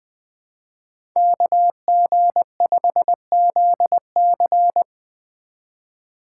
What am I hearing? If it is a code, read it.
Morse code, heard as a keyed tone, about 24 wpm, 700 Hz. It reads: KG5ZC
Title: Speed Racing - USA 2x2 Call Signs 20wpm